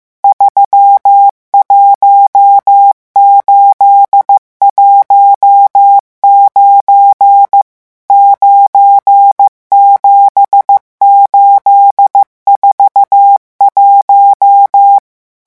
Morse code, heard as a keyed tone, about 15 wpm, 800 Hz. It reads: 31819 97841
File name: morse mp3 / revision11 chiffres vitesse 14